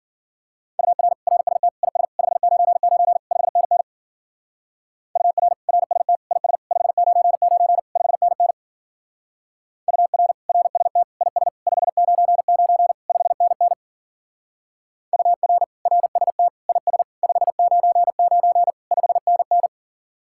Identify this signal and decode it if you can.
Morse code, heard as a keyed tone, about 48 wpm, 700 Hz. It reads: UR RST IS 599 5NN UR RST IS 599 5NN UR RST IS 599 5NN UR RST IS 599 5NN
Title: Speed-Racing Common QSO Phrases - 40wpm